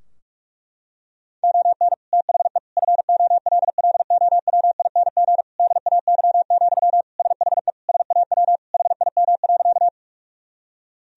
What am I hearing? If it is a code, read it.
Morse code, heard as a keyed tone, about 46 wpm, 700 Hz. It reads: ON THE FOLLOWING DAY, SHE SAW HIM.
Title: Sentences from Top 1000 Words - Review Part 4 - 45wpm